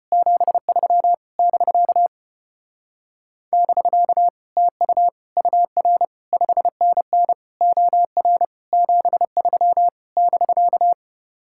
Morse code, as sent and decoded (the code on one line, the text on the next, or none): --... ...-- / -...-.- / -...-.- / - ..- / ..- .-. / ..... -. -. / --- .-. / --... ...-- / -...-.-
73 <BK> <BK> TU UR 5NN OR 73 <BK>